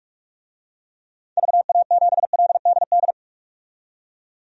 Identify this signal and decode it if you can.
Morse code, heard as a keyed tone, about 43 wpm, 700 Hz. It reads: VA7LDD
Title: DXCC-100 Call Signs - 6 Characters - 45wpm